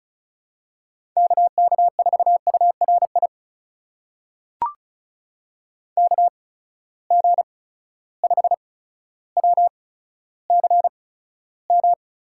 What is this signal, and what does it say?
Morse code, heard as a keyed tone, about 36 wpm, spaced as for 15 wpm, 700 Hz. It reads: KK4URI KG5WCM
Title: USA Call Signs - 6 Characters - 8x Character Spacing - 35wpm